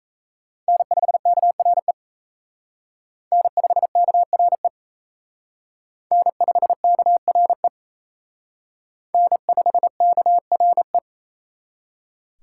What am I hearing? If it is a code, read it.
Morse code, heard as a keyed tone, about 34 wpm, 700 Hz. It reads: N5KRE N5KRE N5KRE N5KRE
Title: Speed Racing - USA 1x3 Call Signs 28wpm